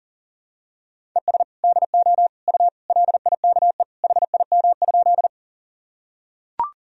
Morse code, heard as a keyed tone, about 40 wpm, 700 Hz. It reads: ES DO U LIKE HIM?
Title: Sentences from Top 100 Words and Abbreviations 40wpm